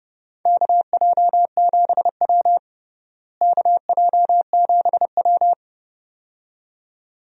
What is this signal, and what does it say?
Morse code, heard as a keyed tone, about 32 wpm, 700 Hz. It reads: KJ7W KJ7W
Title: USA Call Signs - 4 Characters - Repeated 2x - 30wpm